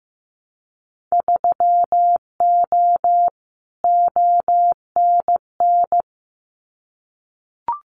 Morse code, heard as a keyed tone, about 15 wpm, 700 Hz. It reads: 3O ONN
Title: Winter Field Day Exchanges 15wpm